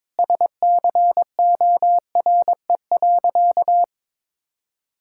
Morse code, heard as a keyed tone, about 22 wpm, 700 Hz. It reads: SCORE.
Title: Sentences from Top 1600 Words 22wpm